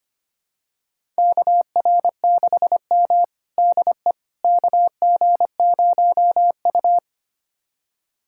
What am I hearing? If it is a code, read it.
Morse code, heard as a keyed tone, about 25 wpm, 700 Hz. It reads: KR6M DE KG0U
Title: USA Callsign DE Callsign - 4 Characters - 25wpm